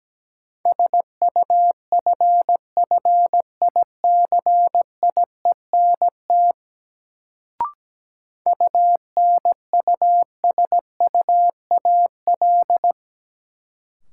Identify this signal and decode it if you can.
Morse code, heard as a keyed tone, about 17 wpm, 700 Hz. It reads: SUFFICIENT UNUSUAL
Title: Sentences from Top 1400 Words - New Word Drill Alt - 17wpm